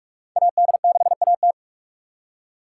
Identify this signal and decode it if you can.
Morse code, heard as a keyed tone, about 43 wpm, 700 Hz. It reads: AD6AT